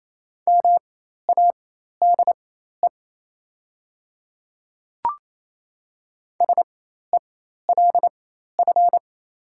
Morse code, heard as a keyed tone, about 27 wpm, spaced as for 13 wpm, 690 Hz. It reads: MADE SELF
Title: 4x Character Spacing - Top 200 Words - 28wpm